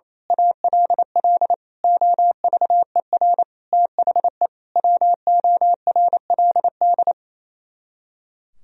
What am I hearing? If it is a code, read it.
Morse code, heard as a keyed tone, about 27 wpm, 700 Hz. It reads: ALL OVER THE WORLD
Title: Speed-Racing - Sets of 5 Words 28wpm